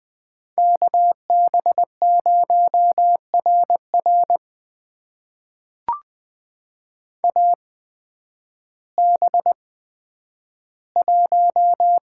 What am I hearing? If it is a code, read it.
Morse code, heard as a keyed tone, about 20 wpm, spaced as for 11 wpm, 700 Hz. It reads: KB0RR AB1
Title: USA Call Signs - 5 Characters - 8x Character Spacing - 20wpm